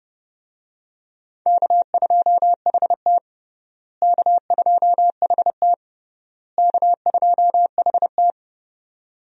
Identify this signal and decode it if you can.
Morse code, heard as a keyed tone, about 30 wpm, 700 Hz. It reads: K2HT K2HT K2HT
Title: USA Call Signs - 4 Characters - Repeated 3x - 30wpm